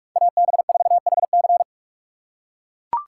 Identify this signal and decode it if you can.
Morse code, heard as a keyed tone, about 40 wpm, 700 Hz. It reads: AB4HC